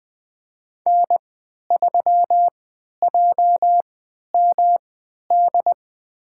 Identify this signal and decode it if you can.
Morse code, heard as a keyed tone, about 20 wpm, spaced as for 11 wpm, 700 Hz. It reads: N3JMD
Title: USA Call Signs - 5 Characters - 3x Character Spacing - 20wpm